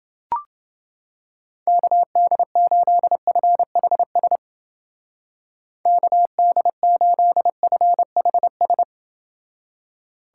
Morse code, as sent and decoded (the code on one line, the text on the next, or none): -.- -.. ---.. ..-. .... ... / -.- -.. ---.. ..-. .... ...
KD8FHS KD8FHS